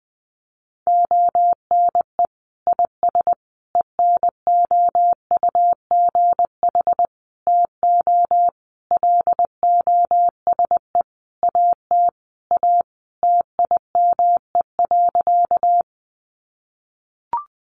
Morse code, heard as a keyed tone, about 20 wpm, 700 Hz. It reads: ONE IS ENOUGH TO LOSE AT A TIME.
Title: Sentences from Top 1100 Words 20wpm